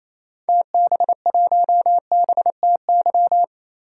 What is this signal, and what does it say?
Morse code, heard as a keyed tone, about 28 wpm, 700 Hz. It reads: TB1BTY
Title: DXCC-100 Call Signs - 6 Characters - 28wpm